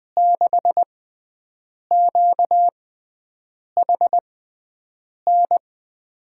Morse code, heard as a keyed tone, about 21 wpm, spaced as for 7 wpm, 700 Hz. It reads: 6QHN
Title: USA Call Signs - 6 Characters - 6x Character Spacing - 20wpm